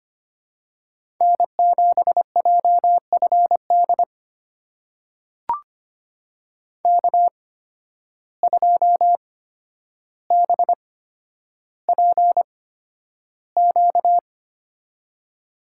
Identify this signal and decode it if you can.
Morse code, heard as a keyed tone, about 25 wpm, spaced as for 11 wpm, 700 Hz. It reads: N7JFD K2BPQ
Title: USA Call Signs - 5 Characters - 8x Character Spacing - 25wpm